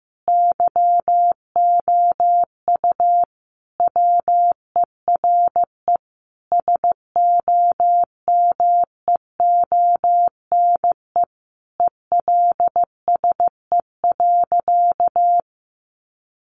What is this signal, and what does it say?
Morse code, heard as a keyed tone, about 14 wpm, 700 Hz. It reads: YOU WERE SOMEONE ELSE.
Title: Sentences from Top 800 Words 15wpm